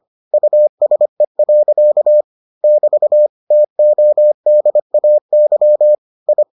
Morse code, heard as a keyed tone, about 25 wpm, 600 Hz. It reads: USE. <BT> TODAY I